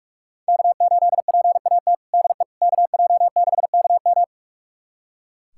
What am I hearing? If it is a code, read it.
Morse code, heard as a keyed tone, about 43 wpm, 700 Hz. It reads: K8PAT DE KJ6KM
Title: USA Callsign DE Callsign - 5 Characters - 45wpm